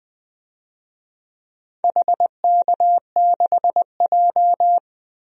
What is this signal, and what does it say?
Morse code, heard as a keyed tone, about 20 wpm, 700 Hz. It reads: HK6J
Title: DXCC-100 Call Signs - 4 Characters - 20wpm